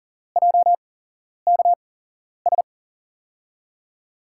Morse code, heard as a keyed tone, about 42 wpm, spaced as for 11 wpm, 700 Hz. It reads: JKS
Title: USA Call Signs - 6 Characters - 8x Character Spacing - 40wpm